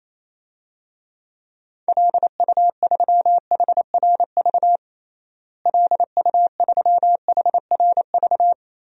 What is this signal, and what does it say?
Morse code, heard as a keyed tone, about 28 wpm, 700 Hz. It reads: LU3HRV LU3HRV
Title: DXCC-100 Call Signs - 6 Characters - Repeated 3x - 28wpm